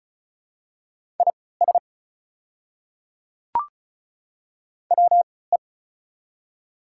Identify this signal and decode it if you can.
Morse code, heard as a keyed tone, about 36 wpm, spaced as for 20 wpm, 690 Hz. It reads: IS WE
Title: 3x Character Spacing - Top 100 Words - 35wpm